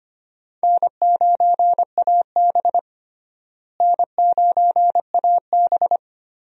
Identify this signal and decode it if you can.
Morse code, heard as a keyed tone, about 25 wpm, 700 Hz. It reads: N9AB N9AB
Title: USA Call Signs - 4 Characters - Repeated 2x - 25wpm